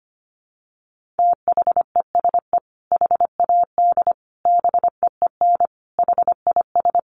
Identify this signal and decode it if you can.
Morse code, heard as a keyed tone, about 25 wpm, 700 Hz. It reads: THESE HAD BEEN HIS
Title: Sentences from Top 1500 Words 25wpm